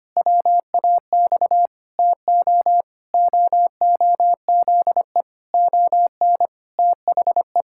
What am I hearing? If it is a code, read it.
Morse code, heard as a keyed tone, about 25 wpm, 700 Hz. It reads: WAX TO OOZE ON THE
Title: Pangram Challenge v2 - 25wpm